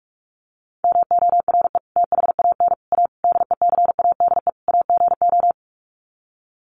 Morse code, heard as a keyed tone, about 44 wpm, 700 Hz. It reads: MORE THAN A DECADE AGO
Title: Sets of 5 Words Encore2 45wpm